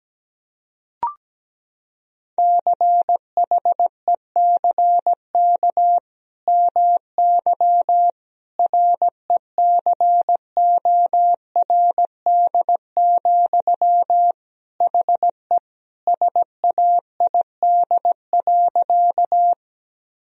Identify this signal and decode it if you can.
Morse code, heard as a keyed tone, about 17 wpm, 700 Hz. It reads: CHECK MY RECORD, HE SAID.
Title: Sentences from Top 1000 Words - Review Part 3 - 17wpm